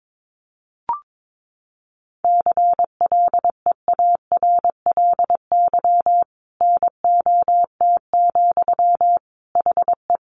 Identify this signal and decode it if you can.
Morse code, heard as a keyed tone, about 22 wpm, 700 Hz. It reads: CLEARLY NOT, HE
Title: Sentences from Top 1300 Words 22wpm